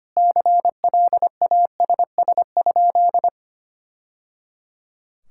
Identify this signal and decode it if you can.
Morse code, heard as a keyed tone, about 25 wpm, 700 Hz. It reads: CLASS?
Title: Sentences from Top 1000 Words - Review Part 5 - 25wpm